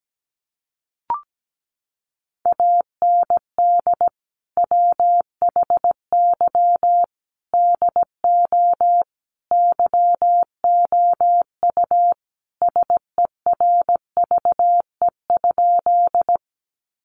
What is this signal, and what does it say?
Morse code, heard as a keyed tone, about 17 wpm, 700 Hz. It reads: AND WHY DO YOU SERVE?
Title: Sentences from Top 400 Words 17wpm